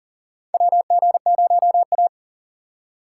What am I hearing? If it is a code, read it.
Morse code, heard as a keyed tone, about 40 wpm, 700 Hz. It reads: WG0A
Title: USA Call Signs - 4 Characters - 4x Character Spacing - 40wpm